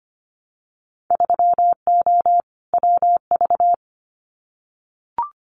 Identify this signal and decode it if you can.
Morse code, heard as a keyed tone, about 25 wpm, 690 Hz. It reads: 3O WV
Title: Winter Field Day Exchanges 25wpm